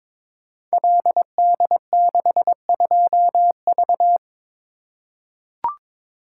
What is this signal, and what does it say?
Morse code, heard as a keyed tone, about 23 wpm, 700 Hz. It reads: LD62V
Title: Code Groups - 5 Letter:Numbers - 22wpm